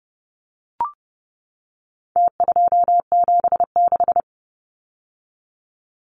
Morse code, heard as a keyed tone, about 30 wpm, 700 Hz. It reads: T276
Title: Numbers - Contest Style - 4-Digit 30wpm